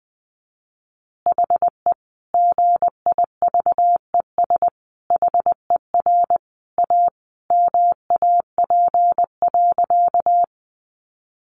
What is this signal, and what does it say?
Morse code, heard as a keyed tone, about 20 wpm, 700 Hz. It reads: HE GIVES HER A MAP.
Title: Additional Easy Phrases 20wpm